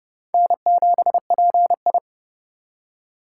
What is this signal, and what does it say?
Morse code, heard as a keyed tone, about 27 wpm, 700 Hz. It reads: N7PI